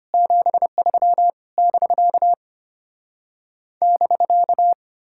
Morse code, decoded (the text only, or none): 73 <BK> <BK>